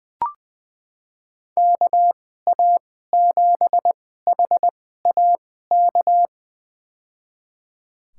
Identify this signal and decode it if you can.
Morse code, heard as a keyed tone, about 20 wpm, spaced as for 15 wpm, 700 Hz. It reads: KA7HAK
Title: USA Call Signs - 2x Character Spacing - 20wpm